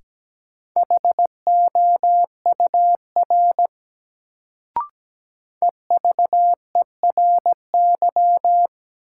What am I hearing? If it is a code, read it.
Morse code, heard as a keyed tone, about 17 wpm, 700 Hz. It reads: HOUR EVERY
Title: Sets of 5 Words - New Word Drill - 17wpm